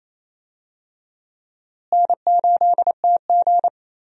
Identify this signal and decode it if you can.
Morse code, heard as a keyed tone, about 28 wpm, 700 Hz. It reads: N8TG
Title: USA Call Signs - 4 Characters - Repeated 3x - 28wpm